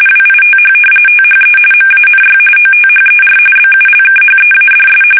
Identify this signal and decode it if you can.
RTTY, about 45.45 baud. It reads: WITH THE MANY OTHER BULKHEADS
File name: fsk850 / battleshipnj